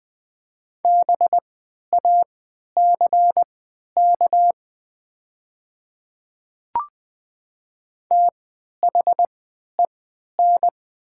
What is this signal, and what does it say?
Morse code, heard as a keyed tone, about 20 wpm, spaced as for 11 wpm, 700 Hz. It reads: BACK THEN
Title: 3x Character Spacing - Top 200 Words - 20wpm